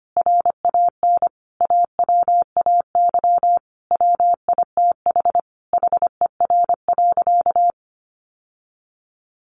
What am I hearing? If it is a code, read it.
Morse code, heard as a keyed tone, about 25 wpm, 700 Hz. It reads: RAN AWAY WITH HER.
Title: Sentences from Top 1500 Words 25wpm